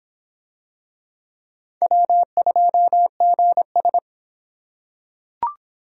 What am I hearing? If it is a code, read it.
Morse code, heard as a keyed tone, about 25 wpm, 700 Hz. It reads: W2GS